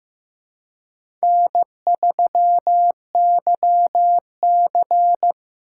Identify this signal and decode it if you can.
Morse code, heard as a keyed tone, about 15 wpm, 700 Hz. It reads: N3YC
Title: USA Call Signs - 4 Characters - 2x Character Spacing - 15wpm